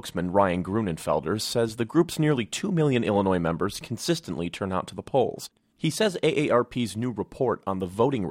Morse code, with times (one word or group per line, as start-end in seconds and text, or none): none